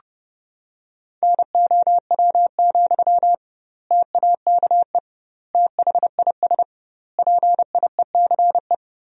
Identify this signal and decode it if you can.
Morse code, heard as a keyed tone, about 30 wpm, 700 Hz. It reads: NOW, TAKE THIS PIECE
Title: 2x Word-Spacing - Sentences from Top 400 Words - 30wpm